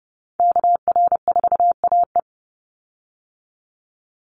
Morse code, heard as a keyed tone, about 29 wpm, 700 Hz. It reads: KR4AE
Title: Call Signs Repeated 2x - 30wpm